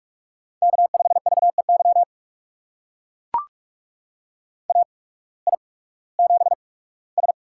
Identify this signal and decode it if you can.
Morse code, heard as a keyed tone, about 44 wpm, spaced as for 21 wpm, 700 Hz. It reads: K5VEY AI7S
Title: USA Call Signs - 8x Character Spacing - 45wpm